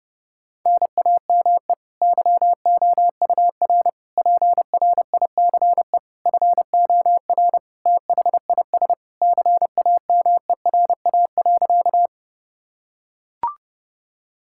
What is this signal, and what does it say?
Morse code, heard as a keyed tone, about 30 wpm, 700 Hz. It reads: NAME YOUR PRICE FOR THIS CAMERA.
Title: Sentences from Top 1000 Words - Review Part 5 - 30wpm